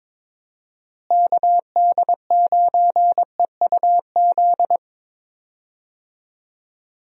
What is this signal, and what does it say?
Morse code, heard as a keyed tone, about 22 wpm, 700 Hz. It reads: KD9EUZ
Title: USA Call Signs - 6 Characters - Repeated 3x - 22wpm